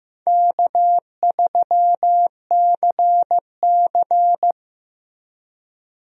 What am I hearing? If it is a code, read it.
Morse code, heard as a keyed tone, about 16 wpm, 700 Hz. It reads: K3CC
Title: USA Call Signs - 4 Characters - Repeated 2x - 15wpm